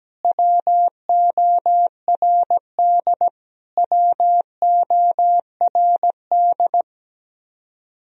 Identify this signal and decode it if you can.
Morse code, heard as a keyed tone, about 17 wpm, 700 Hz. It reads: WORD WORD
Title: Top 100 Words Repeated 17wpm